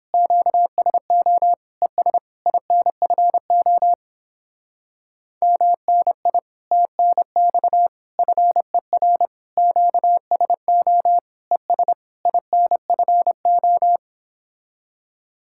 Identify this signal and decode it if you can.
Morse code, heard as a keyed tone, about 29 wpm, 700 Hz. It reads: QSO ES INFO MNI TNX FER QSO ES INFO